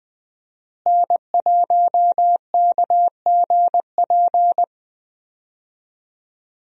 Morse code, heard as a keyed tone, about 20 wpm, 700 Hz. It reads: N1KGP
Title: USA Call Signs - 5 Characters - Repeated 3x - 20wpm